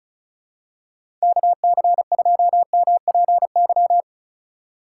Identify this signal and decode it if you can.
Morse code, heard as a keyed tone, about 34 wpm, 700 Hz. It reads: KC2MPY
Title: USA Call Signs - 6 Characters - 2x Character Spacing - 35wpm